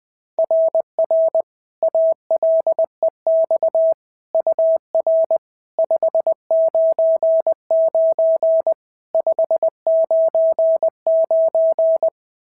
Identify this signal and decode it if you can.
Morse code, heard as a keyed tone, about 20 wpm, 660 Hz. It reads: RR ALEX UR 599 599
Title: SOTA QSO Practice 20wpm